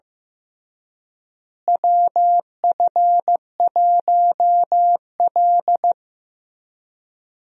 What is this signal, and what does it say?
Morse code, heard as a keyed tone, about 15 wpm, 700 Hz. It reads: WF1L
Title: USA Call Signs - 4 Characters - Repeated 3x - 15wpm